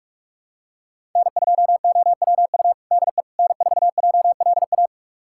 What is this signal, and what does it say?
Morse code, heard as a keyed tone, about 46 wpm, 700 Hz. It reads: N2OWU DE N4JLA
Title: USA Callsign DE Callsign - 5 Characters - 45wpm